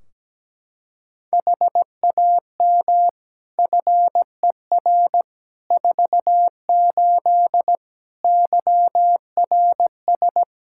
Morse code, heard as a keyed tone, about 17 wpm, 700 Hz. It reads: HAM FER 48 YRS
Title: Practice QSO - Exchange 3 - More Info - 17wpm